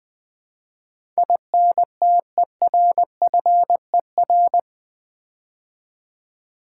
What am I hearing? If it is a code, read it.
Morse code, heard as a keyed tone, about 20 wpm, 700 Hz. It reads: INTERFER
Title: Build A Word - 4 Letters At A Time - 12-Letter Words - 20wpm